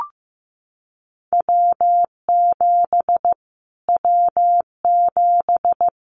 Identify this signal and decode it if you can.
Morse code, heard as a keyed tone, about 15 wpm, 700 Hz. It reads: W7 W7
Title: Callsign Prefixes for Top 25 DXCC - Repeated 3x - 15wpm